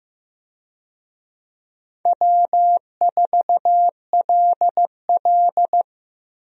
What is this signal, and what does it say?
Morse code, heard as a keyed tone, about 15 wpm, 700 Hz. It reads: W4LL
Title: USA Call Signs - 4 Characters - Repeated 3x - 15wpm